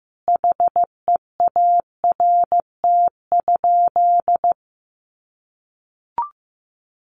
Morse code, heard as a keyed tone, about 15 wpm, 700 Hz. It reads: HEART?
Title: Sentences from Top 1200 Words 15wpm